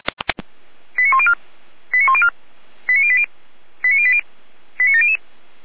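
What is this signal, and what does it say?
Five-tone selective call (ZVEI), heard as quick runs of five tones, tones of about 80 ms. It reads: 89104 89104 89080 89080 89700
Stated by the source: ZVEI five-tone selective call received by SDR.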